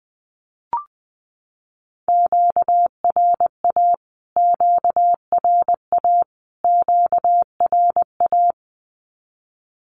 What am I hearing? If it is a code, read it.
Morse code, heard as a keyed tone, about 20 wpm, 700 Hz. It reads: QRA QRA QRA
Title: Q-Codes Repeated 3x 20wpm